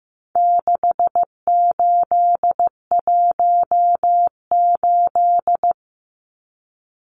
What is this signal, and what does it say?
Morse code, heard as a keyed tone, about 15 wpm, 700 Hz. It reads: 6818
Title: Numbers - 4-Digit 15wpm